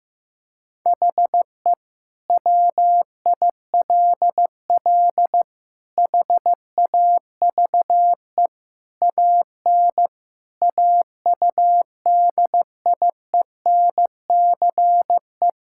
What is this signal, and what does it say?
Morse code, heard as a keyed tone, about 15 wpm, 700 Hz. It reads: HE WILL HAVE AN AUDIENCE
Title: Sentences from Top 1000 Words - Review Part 5 - 15wpm